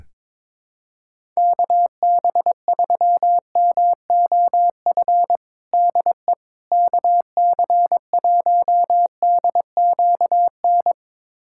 Morse code, heard as a keyed tone, about 22 wpm, 700 Hz. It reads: KB3MOF DE KC1DQN
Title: USA Callsign DE Callsign 22wpm